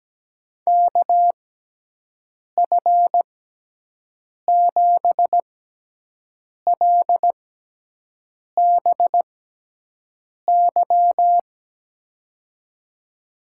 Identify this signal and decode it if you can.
Morse code, heard as a keyed tone, about 17 wpm, spaced as for 6 wpm, 700 Hz. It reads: KF7LBY
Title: USA Call Signs - 6 Characters - 6x Character Spacing - 17wpm